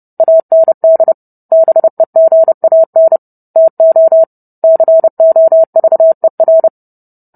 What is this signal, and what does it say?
Morse code, heard as a keyed tone, about 30 wpm, 650 Hz. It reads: AND BEGAN TO COVER